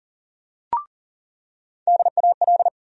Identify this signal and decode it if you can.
Morse code, heard as a keyed tone, about 40 wpm, 700 Hz. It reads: DAL
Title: USA Call Sign Postfixes - 3 Letters 40wpm